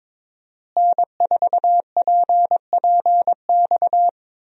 Morse code, heard as a keyed tone, about 22 wpm, 700 Hz. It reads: N4PPX
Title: USA Call Signs - 5 Characters - Repeated 2x - 22wpm